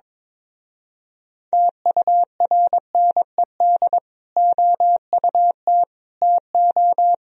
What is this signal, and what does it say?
Morse code, heard as a keyed tone, about 22 wpm, 700 Hz. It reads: TURNED OUT TO
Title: Speed-Racing - Sets of 5 Words 22wpm